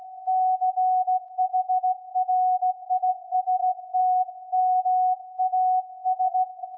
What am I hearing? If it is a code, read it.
Morse code, heard as a keyed tone, about 16 wpm, 740 Hz. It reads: CHRISTMAS
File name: iz7slz / VLF / SAQ 24-12-2021